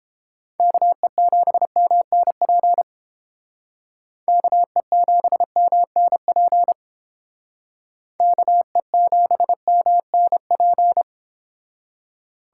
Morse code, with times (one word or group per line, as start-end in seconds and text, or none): 0.6-2.8 KE7MNP
4.3-6.7 KE7MNP
8.2-11.0 KE7MNP